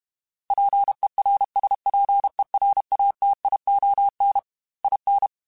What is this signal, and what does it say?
Morse code, heard as a keyed tone, about 32 wpm, 800 Hz. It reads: PERSPERATION IN
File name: Quotes / Quotes-mp3s / Persperation @30